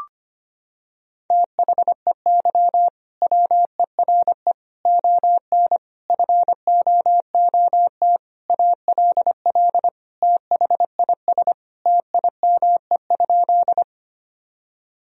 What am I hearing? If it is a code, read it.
Morse code, heard as a keyed tone, about 25 wpm, 700 Hz. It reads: THEY WERE ON FOOT ALL THIS TIME?
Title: Sentences from Top 1000 Words - Review Part 3 - 25wpm